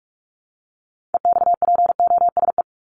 Morse code, heard as a keyed tone, about 44 wpm, 700 Hz. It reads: EXPOSE
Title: Top 1501-1750 Words 45wpm